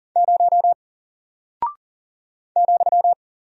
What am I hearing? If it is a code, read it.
Morse code, heard as a keyed tone, about 40 wpm, 700 Hz. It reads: 0 ,
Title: Single Character (Letter:Number:Punctuation:Prosign) Alt 40wpm